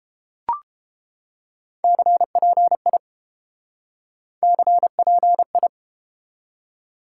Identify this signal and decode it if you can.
Morse code, heard as a keyed tone, about 30 wpm, 700 Hz. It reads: CPI CPI